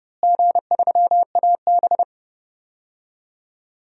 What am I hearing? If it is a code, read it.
Morse code, heard as a keyed tone, about 30 wpm, 700 Hz. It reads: G3AB
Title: DXCC-100 Call Signs - 4 Characters - Repeated 3x - 30wpm